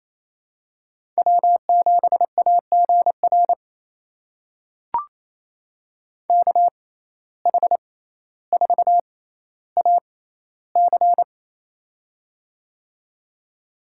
Morse code, heard as a keyed tone, about 28 wpm, spaced as for 14 wpm, 700 Hz. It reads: W7AGR KH4AC
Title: USA Call Signs - 5 Characters - 6x Character Spacing - 28wpm